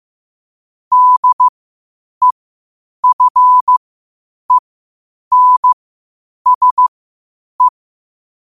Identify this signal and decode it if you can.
Morse code, heard as a keyed tone, about 15 wpm, spaced as for 8 wpm, 1000 Hz. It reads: DEFENSE
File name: qotdfiles / 8wpm